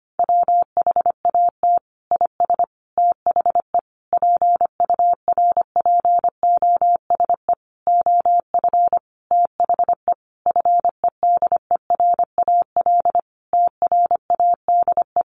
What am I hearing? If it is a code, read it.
Morse code, heard as a keyed tone, about 25 wpm, 700 Hz. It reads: WHAT IS THE PURPOSE OF THE FEDERAL TRADE
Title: Sentences from Top 1400 Words 25wpm